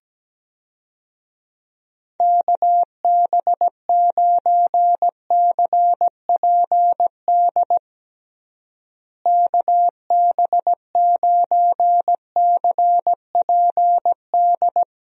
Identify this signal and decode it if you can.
Morse code, heard as a keyed tone, about 17 wpm, 700 Hz. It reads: KB9CPD KB9CPD
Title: USA Call Signs - 6 Characters - Repeated 3x - 17wpm